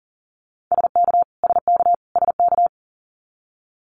Morse code, heard as a keyed tone, about 40 wpm, 700 Hz. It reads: SK SK SK
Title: Canadian Provinces and Territories - Repeated 3x - 40wpm